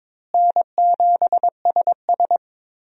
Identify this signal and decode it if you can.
Morse code, heard as a keyed tone, about 22 wpm, 700 Hz. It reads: N7SS
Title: USA Call Signs - 4 Characters - Repeated 2x - 22wpm